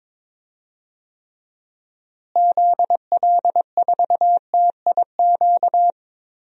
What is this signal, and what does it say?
Morse code, heard as a keyed tone, about 22 wpm, 700 Hz. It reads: ZL4TIQ
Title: DXCC-100 Call Signs - 6 Characters - Repeated 3x - 22wpm